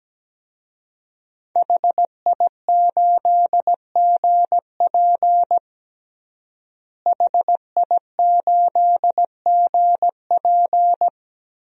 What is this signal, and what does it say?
Morse code, heard as a keyed tone, about 17 wpm, 700 Hz. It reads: HI8GP HI8GP
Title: DXCC-100 Call Signs - 5 Characters - Repeated 2x - 17wpm